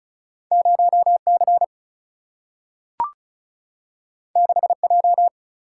Morse code, heard as a keyed tone, about 36 wpm, 700 Hz. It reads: 0 C 6 J
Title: Single-Letter Single-Number 35wpm